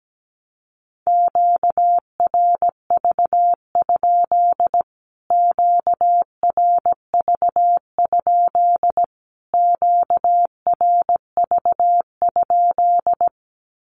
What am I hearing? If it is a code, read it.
Morse code, heard as a keyed tone, about 17 wpm, 700 Hz. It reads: QRV? QRV? QRV?
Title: Q-Codes Repeated 3x 17wpm